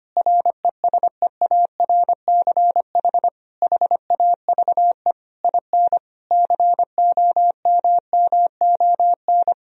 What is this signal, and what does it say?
Morse code, heard as a keyed tone, about 25 wpm, 700 Hz. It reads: RESEARCH HAVE IN COMMON
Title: Sentences from Top 1000 Words - Review Part 4 - 25wpm